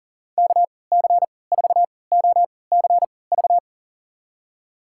Morse code, heard as a keyed tone, about 40 wpm, spaced as for 23 wpm, 700 Hz. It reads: KC4OCV
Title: USA Call Signs - 6 Characters - 3x Character Spacing - 40wpm